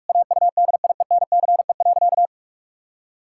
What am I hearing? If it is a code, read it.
Morse code, heard as a keyed tone, about 41 wpm, 700 Hz. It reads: AUDIENCE.
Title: Pangram Challenge v2 - 45wpm